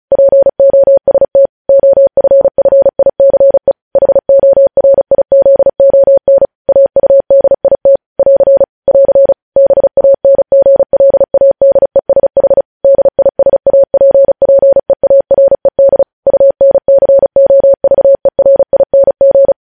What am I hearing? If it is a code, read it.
Morse code, heard as a keyed tone, about 36 wpm, 550 Hz. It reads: POST OFFICE HORIZON AUDIT <AR> <AR> BANGLADESH DISAPPEARED UNCOVERING
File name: podcast / News headlines-1x-35x35